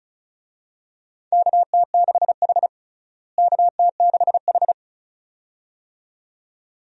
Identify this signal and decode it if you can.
Morse code, heard as a keyed tone, about 36 wpm, 700 Hz. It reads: KT6H KT6H
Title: USA Call Signs - 4 Characters - Repeated 2x - 35wpm